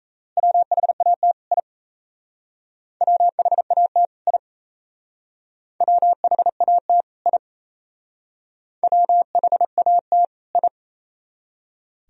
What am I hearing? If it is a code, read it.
Morse code, heard as a keyed tone, about 37 wpm, 700 Hz. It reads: WHAT I WHAT I WHAT I WHAT I